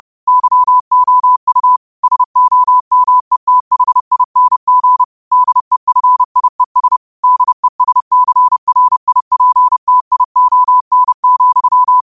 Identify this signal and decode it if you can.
Morse code, heard as a keyed tone, about 30 wpm, 1000 Hz. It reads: YOU SOMETHING DEFIES DESCRIPTION,
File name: qotdfiles / 30wpm